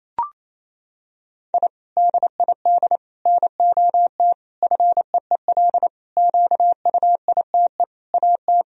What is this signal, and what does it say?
Morse code, heard as a keyed tone, about 28 wpm, 700 Hz. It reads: I DID NOT FEEL QUITE AT
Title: Sentences from Top 1000 Words - Review Part 1 - 28wpm